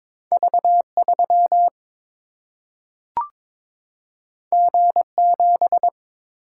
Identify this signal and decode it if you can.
Morse code, heard as a keyed tone, about 22 wpm, 700 Hz. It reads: V 3 G 7
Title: Single-Letter Single-Number 22wpm